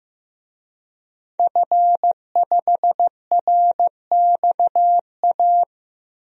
Morse code, heard as a keyed tone, about 15 wpm, 700 Hz. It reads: F5RXA
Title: DXCC-100 Call Signs - 5 Characters - 15wpm